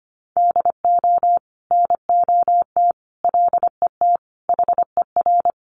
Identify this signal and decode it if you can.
Morse code, heard as a keyed tone, about 25 wpm, 700 Hz. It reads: DO NOT LET HER
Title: Sentences from Top 1300 Words 25wpm